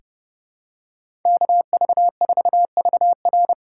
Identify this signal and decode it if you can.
Morse code, heard as a keyed tone, about 31 wpm, 700 Hz. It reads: KV4VR
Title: USA Call Signs - 4x Character Spacing - 30wpm